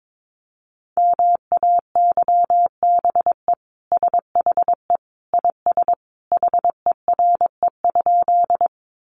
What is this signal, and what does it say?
Morse code, heard as a keyed tone, about 22 wpm, 700 Hz. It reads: MAYBE SHE IS HERE?
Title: Sentences from Top 1000 Words 22wpm